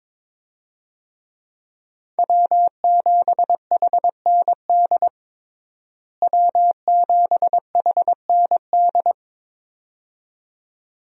Morse code, heard as a keyed tone, about 22 wpm, 700 Hz. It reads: W7HND W7HND
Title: USA Call Signs - 5 Characters - Repeated 2x - 22wpm